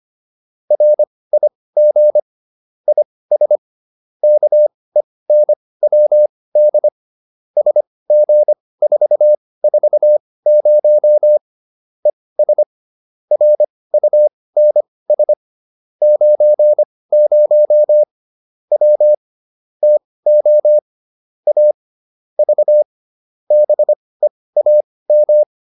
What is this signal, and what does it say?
Morse code, heard as a keyed tone, about 25 wpm, spaced as for 18 wpm, 600 Hz. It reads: RIG IS KENWD SG440 ES RUNS 90 W TO A V BEAM